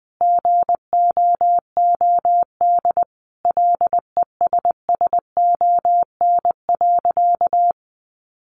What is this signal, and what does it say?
Morse code, heard as a keyed tone, about 19 wpm, 700 Hz. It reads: GOOD LESSON.
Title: Sentences from Top 1200 Words 20wpm